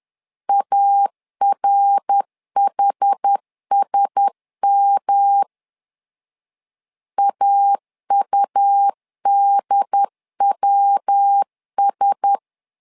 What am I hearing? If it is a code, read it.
Morse code, heard as a keyed tone, about 10 wpm, 800 Hz. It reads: ARHSM AUDWS
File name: LEZIONI / 4 Q1S 10wpm 2133